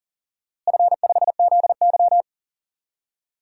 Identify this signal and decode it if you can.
Morse code, heard as a keyed tone, about 40 wpm, 700 Hz. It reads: F5ZY
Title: DXCC-100 Call Signs - 4 Characters - Repeated 2x - 40wpm